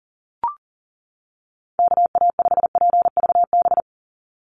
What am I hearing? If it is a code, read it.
Morse code, heard as a keyed tone, about 40 wpm, 700 Hz. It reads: KA5PVB
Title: NA SOTA Chaser Callsigns 40wpm